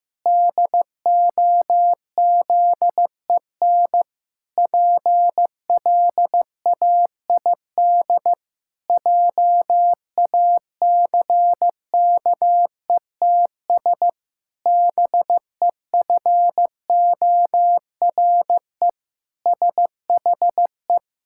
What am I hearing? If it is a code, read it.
Morse code, heard as a keyed tone, about 15 wpm, 700 Hz. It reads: DOZEN PLAID JACKETS BEFORE SHE
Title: Pangram Challenge v2 - 15wpm